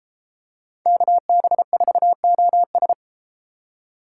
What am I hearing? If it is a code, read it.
Morse code, heard as a keyed tone, about 32 wpm, 700 Hz. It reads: KB4OS